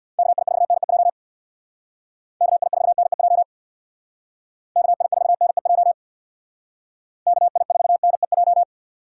Morse code, heard as a keyed tone, about 63 wpm, 700 Hz. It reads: KI4NEJ KI4NEJ KI4NEJ KI4NEJ